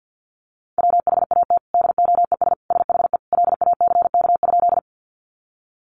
Morse code, heard as a keyed tone, about 50 wpm, 700 Hz. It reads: WHAT DOES SHE LACK?
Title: Sentences from Top 1100 Words 50wpm